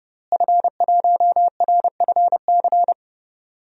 Morse code, heard as a keyed tone, about 32 wpm, 700 Hz. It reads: F1RFC
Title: DXCC-100 Call Signs - 5 Characters - Repeated 3x - 30wpm